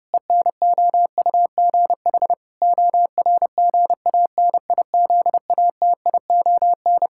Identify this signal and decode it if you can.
Morse code, heard as a keyed tone, about 30 wpm, 700 Hz. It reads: ENOUGH ORGANIZATION
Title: Sentences from Top 1000 Words - Review Part 5 - 30wpm